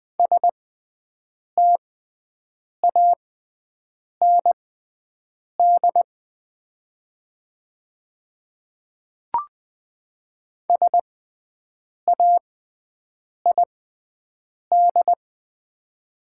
Morse code, heard as a keyed tone, about 20 wpm, spaced as for 7 wpm, 700 Hz. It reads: STAND SAID
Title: 6x Character Spacing - Top 500 Words - 20wpm